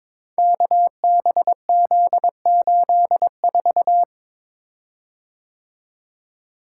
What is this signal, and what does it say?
Morse code, heard as a keyed tone, about 22 wpm, 700 Hz. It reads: KBZ84
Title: Code Groups - 5 Letter:Numbers - 22wpm